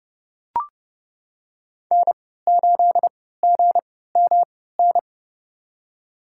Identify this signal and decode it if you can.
Morse code, heard as a keyed tone, about 30 wpm, spaced as for 17 wpm, 700 Hz. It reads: N8GMN
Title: USA Call Signs - 5 Characters - 3x Character Spacing - 30wpm